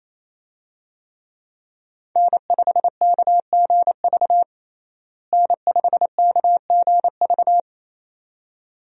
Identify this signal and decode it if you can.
Morse code, heard as a keyed tone, about 28 wpm, 700 Hz. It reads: N5KGV N5KGV
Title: USA Call Signs - 5 Characters - Repeated 2x - 28wpm